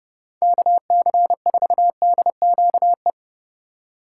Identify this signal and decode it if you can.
Morse code, heard as a keyed tone, about 31 wpm, 700 Hz. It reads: KC4DQE